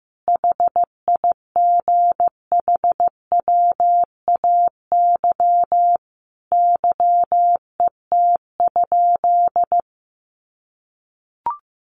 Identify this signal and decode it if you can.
Morse code, heard as a keyed tone, about 15 wpm, 700 Hz. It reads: HIGHWAY YET?
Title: Sentences from Top 1400 Words 15wpm